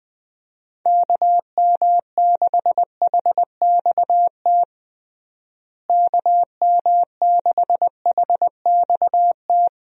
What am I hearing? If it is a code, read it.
Morse code, heard as a keyed tone, about 21 wpm, 700 Hz. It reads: KM6HXT KM6HXT
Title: USA Call Signs - 6 Characters - Repeated 2x - 20wpm